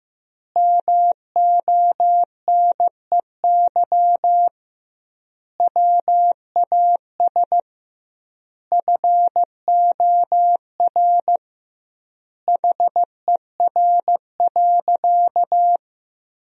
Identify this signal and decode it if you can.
Morse code, heard as a keyed tone, about 15 wpm, 700 Hz. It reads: MONEY WAS FOR HER.
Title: 2x Word-Spacing - Sentences from Top 400 Words - 15wpm